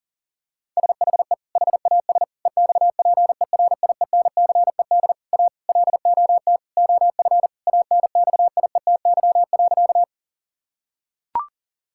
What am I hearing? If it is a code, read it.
Morse code, heard as a keyed tone, about 37 wpm, 700 Hz. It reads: SHE HAS EXPERIENCED A LOT OF ANXIETY.
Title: Sentences from Top 1600 Words 40wpm